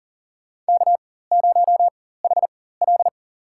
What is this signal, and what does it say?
Morse code, heard as a keyed tone, about 37 wpm, spaced as for 18 wpm, 700 Hz. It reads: K0HL
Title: USA Call Signs - 4 Characters - 4x Character Spacing - 40wpm